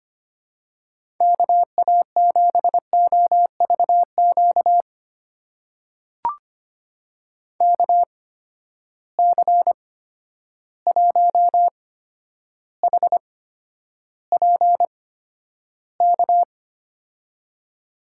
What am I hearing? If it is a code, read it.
Morse code, heard as a keyed tone, about 25 wpm, spaced as for 11 wpm, 700 Hz. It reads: KA7OVQ KC1HPK
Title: USA Call Signs - 6 Characters - 8x Character Spacing - 25wpm